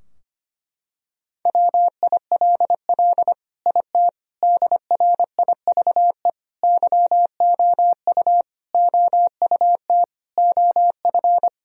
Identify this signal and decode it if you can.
Morse code, heard as a keyed tone, about 25 wpm, 700 Hz. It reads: WILL IT DRIVE YOU OUT OF
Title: Sentences from Top 1000 Words - Review Part 2 - 25wpm